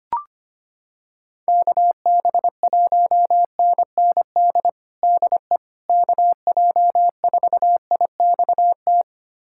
Morse code, heard as a keyed tone, about 25 wpm, 700 Hz. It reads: KB1NND DE KJ4IXT
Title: USA Callsign DE Callsign - 6 Characters - 25wpm